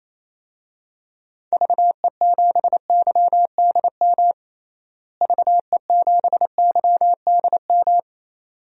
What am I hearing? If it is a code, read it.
Morse code, heard as a keyed tone, about 27 wpm, 700 Hz. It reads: VE7YDM VE7YDM
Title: DXCC-100 Call Signs - 6 Characters - Repeated 2x - 28wpm